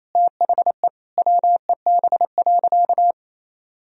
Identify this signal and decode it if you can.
Morse code, heard as a keyed tone, about 28 wpm, 700 Hz. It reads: THE WEB.
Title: Sentences from Top 1000 Words - Review Part 4 - 28wpm